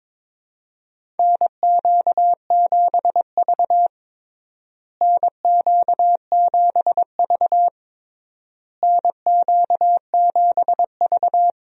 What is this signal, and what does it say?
Morse code, heard as a keyed tone, about 22 wpm, 700 Hz. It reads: NQ7V NQ7V NQ7V
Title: USA Call Signs - 4 Characters - Repeated 3x - 22wpm